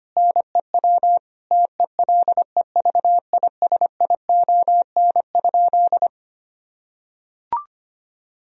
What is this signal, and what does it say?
Morse code, heard as a keyed tone, about 25 wpm, 700 Hz. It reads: NEW TELEVISION?
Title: Sentences from Top 1000 Words - Review Part 4 - 25wpm